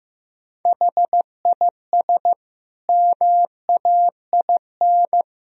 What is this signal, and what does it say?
Morse code, heard as a keyed tone, about 16 wpm, 700 Hz. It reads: HIS MAIN
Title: Sentences from Top 1000 Words - Review Part 2 - 15wpm